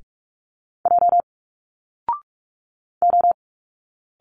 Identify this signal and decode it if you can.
Morse code, heard as a keyed tone, about 46 wpm, 700 Hz. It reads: J O
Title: Single Letters 45wpm